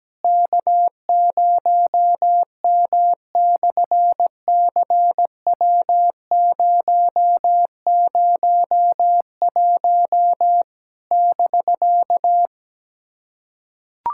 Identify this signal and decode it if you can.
Morse code, heard as a keyed tone, about 17 wpm, 700 Hz. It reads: K0M/CW001 <BK>
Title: SOTA QSO Practice 17wpm